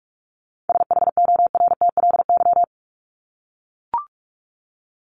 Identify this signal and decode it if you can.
Morse code, heard as a keyed tone, about 44 wpm, 700 Hz. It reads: SHORTLY
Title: LY Suffix - Top 100 Words with Base Words 45wpm